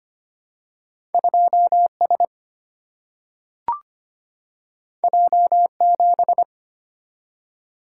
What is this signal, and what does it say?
Morse code, heard as a keyed tone, about 25 wpm, 700 Hz. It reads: 2S J7
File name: Two Letter-Number Combinations - 25wpm